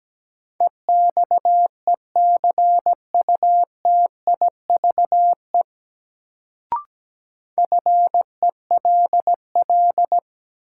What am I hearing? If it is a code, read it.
Morse code, heard as a keyed tone, about 17 wpm, 700 Hz. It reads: EXECUTIVE FELL
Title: Sets of 5 Words Encore - New Word Drill - 17wpm